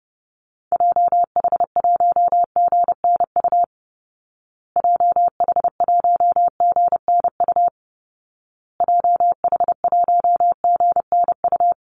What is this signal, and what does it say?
Morse code, heard as a keyed tone, about 30 wpm, 700 Hz. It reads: JH1GNU JH1GNU JH1GNU
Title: Call Signs Repeated 2x - 30wpm